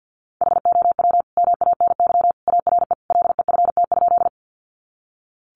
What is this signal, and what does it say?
Morse code, heard as a keyed tone, about 50 wpm, 700 Hz. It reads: HOW MANY ARE LEFT?
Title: Sentences from Top 300 Words 50wpm